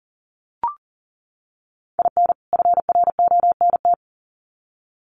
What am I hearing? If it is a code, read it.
Morse code, heard as a keyed tone, about 40 wpm, 700 Hz. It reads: IN FRONT
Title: Sets of 2 Words 40wpm